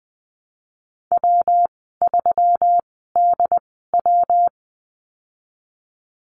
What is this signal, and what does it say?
Morse code, heard as a keyed tone, about 20 wpm, spaced as for 14 wpm, 700 Hz. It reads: W3DW
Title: 128 Lesson 21g - Number 6 - Call Signs 20wpm